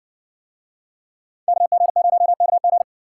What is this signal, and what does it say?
Morse code, heard as a keyed tone, about 57 wpm, 700 Hz. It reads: KG0KG